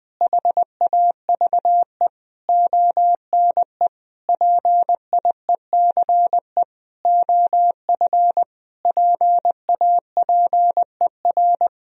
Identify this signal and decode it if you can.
Morse code, heard as a keyed tone, about 20 wpm, 700 Hz. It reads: HAVE ONE PIECE OF PAPER
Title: Sentences from Top 1000 Words - Review Part 2 - 20wpm